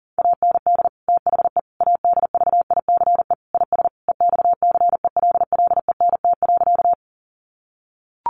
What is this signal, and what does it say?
Morse code, heard as a keyed tone, about 40 wpm, 700 Hz. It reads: AND THE ADVICE IS EXCELLENT.
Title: Sentences from Top 1100 Words 40wpm